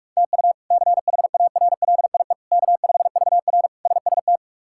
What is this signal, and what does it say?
Morse code, heard as a keyed tone, about 43 wpm, 700 Hz. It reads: TU CHARLIE K5VR SST
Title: K1USN SST QSO Practice - 45wpm